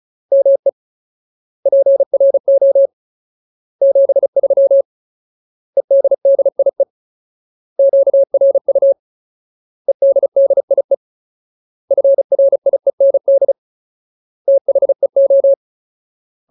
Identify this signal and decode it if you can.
Morse code, heard as a keyed tone, about 34 wpm, 550 Hz. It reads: ME PRO 73 EDDIE QRU EDDIE FRIEND THEO
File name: Ultimate Word List / 35WPM Ultimate